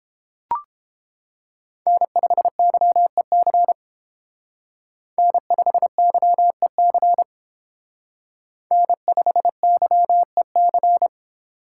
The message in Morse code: N5YEC N5YEC N5YEC

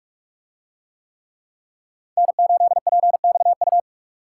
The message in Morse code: N8PXU

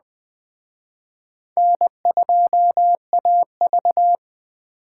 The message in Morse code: N2AV

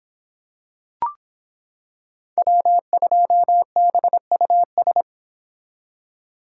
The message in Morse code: W2BUS

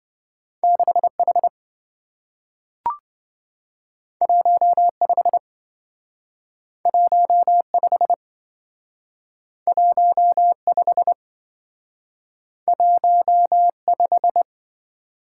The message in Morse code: 6H 15 15 15 15

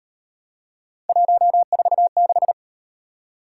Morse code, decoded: 146

